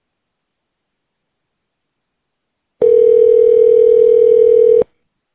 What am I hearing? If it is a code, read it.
Ringback tone (North America), 440 + 480 Hz.